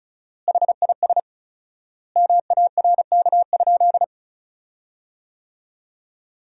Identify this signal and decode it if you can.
Morse code, heard as a keyed tone, about 36 wpm, 700 Hz. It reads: HIS MARK?
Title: 4x Word-Spacing - Sentences from Top 300 Words - 35wpm